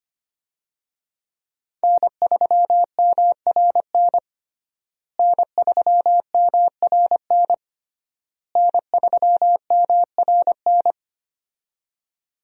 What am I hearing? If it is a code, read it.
Morse code, heard as a keyed tone, about 25 wpm, 700 Hz. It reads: N3MRN N3MRN N3MRN
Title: USA Call Signs - 5 Characters - Repeated 3x - 25wpm